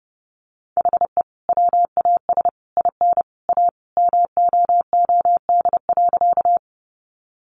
Morse code, heard as a keyed tone, about 28 wpm, 700 Hz. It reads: HE WAS IN A MOOD.
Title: Sentences from Top 1100 Words 30wpm